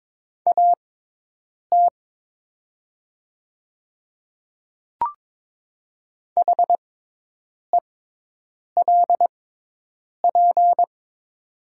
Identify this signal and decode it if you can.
Morse code, heard as a keyed tone, about 22 wpm, spaced as for 8 wpm, 700 Hz. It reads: AT HELP
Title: 6x Character Spacing - Top 200 Words - 22wpm